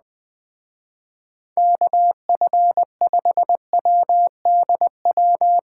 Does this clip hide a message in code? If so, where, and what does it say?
Yes: Morse code from 1.6 to 5.6 s, heard as a keyed tone, about 20 wpm, 700 Hz. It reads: KF5WDW